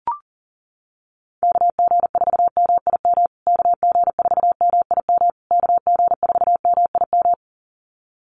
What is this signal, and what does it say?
Morse code, heard as a keyed tone, about 40 wpm, 700 Hz. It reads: KG4MIM KG4MIM KG4MIM
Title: USA Call Signs Repeated 3x 40wpm